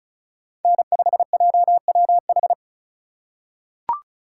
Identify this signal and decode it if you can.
Morse code, heard as a keyed tone, about 36 wpm, 700 Hz. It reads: N5JWH
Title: USA Call Signs - 5 Characters - 2x Character Spacing - 35wpm